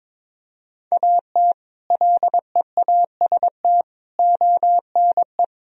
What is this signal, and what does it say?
Morse code, heard as a keyed tone, about 21 wpm, 700 Hz. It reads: AT LEAST ONE
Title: Sentences from Top 1000 Words - Review Part 1 - 22wpm